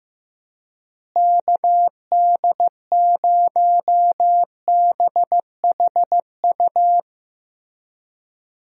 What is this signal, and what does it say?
Morse code, heard as a keyed tone, about 15 wpm, 700 Hz. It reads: KD0BHU
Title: USA Call Signs - 6 Characters - Repeated 2x - 15wpm